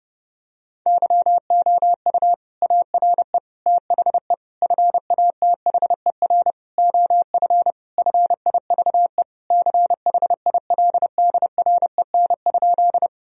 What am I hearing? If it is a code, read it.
Morse code, heard as a keyed tone, about 30 wpm, 700 Hz. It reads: YOU ARE THE FATHER OF FIVE CHILDREN?
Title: Sentences from Top 1000 Words - Review Part 2 - 30wpm